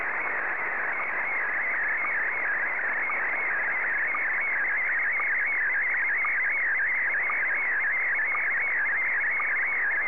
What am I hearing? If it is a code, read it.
SSTV (slow-scan television), Scottie DX.
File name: sstv / SDX-VK3AOB